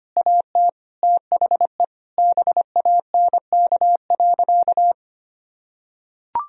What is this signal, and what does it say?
Morse code, heard as a keyed tone, about 25 wpm, 700 Hz. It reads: AT THE BANK.
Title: Sentences from Top 1000 Words - Review Part 4 - 25wpm